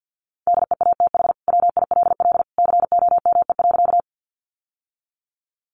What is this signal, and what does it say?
Morse code, heard as a keyed tone, about 50 wpm, 700 Hz. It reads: DEATH WILL COME.
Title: Sentences from Top 1100 Words 50wpm